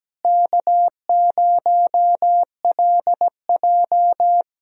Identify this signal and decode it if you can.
Morse code, heard as a keyed tone, about 17 wpm, 700 Hz. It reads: K0LJ
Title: USA Callsign DE Callsign - 4 Characters - 17wpm